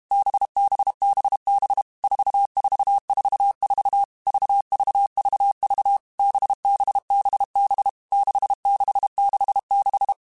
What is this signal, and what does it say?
Morse code, heard as a keyed tone, about 32 wpm, 800 Hz. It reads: BBBB 4444 VVVV BBBB 6666
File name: Groups@30 / Groups@30 mp3s / VB46@30